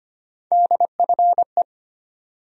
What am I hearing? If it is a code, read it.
Morse code, heard as a keyed tone, about 25 wpm, 700 Hz. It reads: DFE
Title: USA Call Sign Postfixes - 3 Letters 25wpm